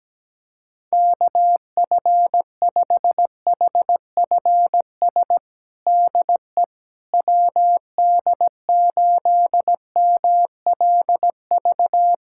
Morse code, heard as a keyed tone, about 17 wpm, 700 Hz. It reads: KF5HFS DE WD8MLV
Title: USA Callsign DE Callsign - 6 Characters - 17wpm